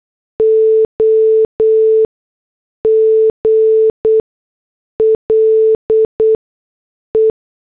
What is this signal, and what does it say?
Morse code, heard as a keyed tone, about 8 wpm, spaced as for 6 wpm, 440 Hz. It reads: OGLE
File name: morse / headlines